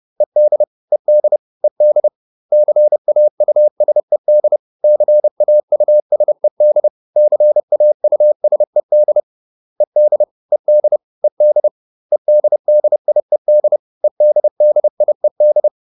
Morse code, heard as a keyed tone, about 30 wpm, 600 Hz. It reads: ED ED ED CAUSED CAUSED CAUSED ED ED ED EDDIED EDDIED